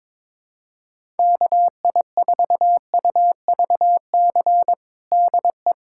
Morse code, heard as a keyed tone, about 22 wpm, 700 Hz. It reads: KI4UVC DE
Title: USA Callsign DE Callsign - 6 Characters - 22wpm